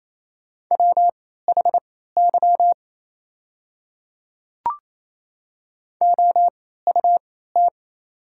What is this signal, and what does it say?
Morse code, heard as a keyed tone, about 27 wpm, spaced as for 16 wpm, 700 Hz. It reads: WHY OUT
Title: 3x Character Spacing - Top 300 Words - 28wpm